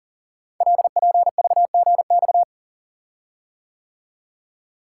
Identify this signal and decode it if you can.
Morse code, heard as a keyed tone, about 42 wpm, 690 Hz. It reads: LPVGX
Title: Code Groups - 5 Letters - 40wpm